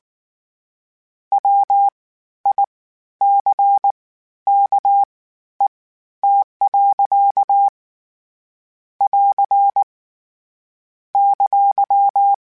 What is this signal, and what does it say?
Morse code, heard as a keyed tone, about 19 wpm, spaced as for 12 wpm, 800 Hz. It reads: WICKET. <AR> !
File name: Misc / Misc-mp3s / WordPairs sp @18